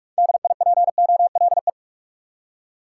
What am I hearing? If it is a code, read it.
Morse code, heard as a keyed tone, about 46 wpm, 700 Hz. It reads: DIPOLE
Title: Top 90 QSO Elements 45wpm